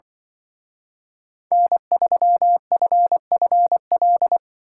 Morse code, heard as a keyed tone, about 24 wpm, 700 Hz. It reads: N3FFL
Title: Speed Racing - USA 1x3 Call Signs 20wpm